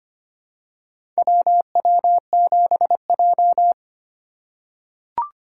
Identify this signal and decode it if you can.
Morse code, heard as a keyed tone, about 25 wpm, 700 Hz. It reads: WW7J
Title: USA Call Signs - 4x Character Spacing - 25wpm